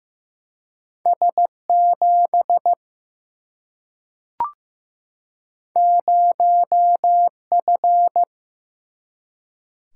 Morse code, heard as a keyed tone, about 15 wpm, 700 Hz. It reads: S7 0F
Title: Single-Letter Single-Number 15wpm